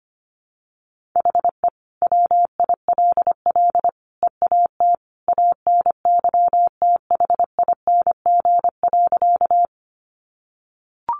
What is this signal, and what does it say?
Morse code, heard as a keyed tone, about 25 wpm, 700 Hz. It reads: HE WILL EAT ANYTHING.
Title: Sentences from Top 1000 Words 25wpm